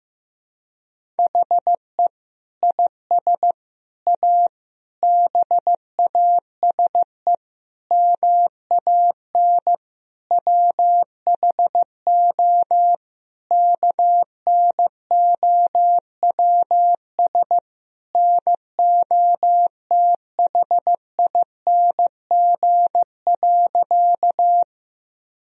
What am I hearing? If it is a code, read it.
Morse code, heard as a keyed tone, about 15 wpm, 700 Hz. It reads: HE IS A BASE MAN WHO KNOWS NOTHING.
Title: Sentences from Top 1000 Words - Review Part 1 - 15wpm